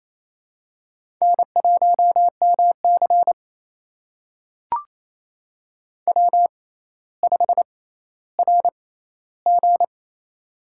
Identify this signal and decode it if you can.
Morse code, heard as a keyed tone, about 28 wpm, spaced as for 14 wpm, 700 Hz. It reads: N1MC W5RG
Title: USA Call Signs - 4 Characters - 6x Character Spacing - 28wpm